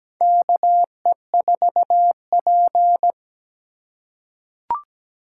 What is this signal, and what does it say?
Morse code, heard as a keyed tone, about 17 wpm, 700 Hz. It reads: KE4P
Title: USA Call Signs - 4 Characters - 3x Character Spacing - 17wpm